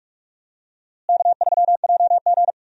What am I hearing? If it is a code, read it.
Morse code, heard as a keyed tone, about 43 wpm, 700 Hz. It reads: K3JG
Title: USA Call Signs - 4 Characters - Repeated 3x - 45wpm